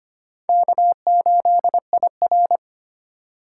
Morse code, heard as a keyed tone, about 25 wpm, 700 Hz. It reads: K8IR
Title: USA Call Signs - 4 Characters - 3x Character Spacing - 25wpm